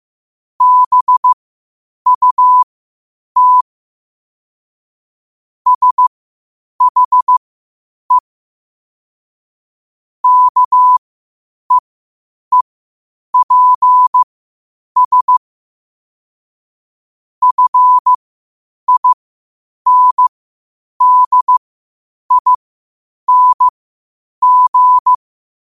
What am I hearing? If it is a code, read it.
Morse code, heard as a keyed tone, about 15 wpm, spaced as for 8 wpm, 1000 Hz. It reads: BUT SHE KEEPS FINDING
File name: qotdfiles / 8wpm